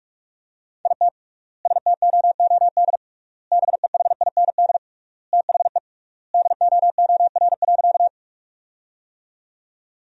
Morse code, heard as a keyed tone, about 46 wpm, 700 Hz. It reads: IT STOOD BEHIND THE DOOR.
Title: 3x Word-Spacing - Sentences from Top 500 Words - 45wpm